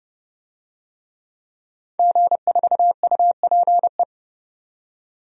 Morse code, heard as a keyed tone, about 30 wpm, 700 Hz. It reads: G4UPE
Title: DXCC-100 Call Signs - 5 Characters - 30wpm